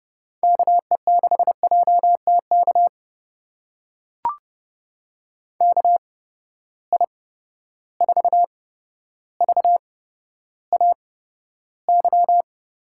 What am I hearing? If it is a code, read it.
Morse code, heard as a keyed tone, about 31 wpm, spaced as for 13 wpm, 700 Hz. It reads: KE6JTK KI4VAY
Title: USA Call Signs - 6 Characters - 8x Character Spacing - 30wpm